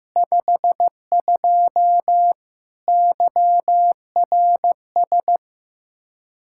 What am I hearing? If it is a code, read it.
Morse code, heard as a keyed tone, about 15 wpm, 700 Hz. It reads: 52 YRS
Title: Practice QSO - Exchange 3 - More Info - 15wpm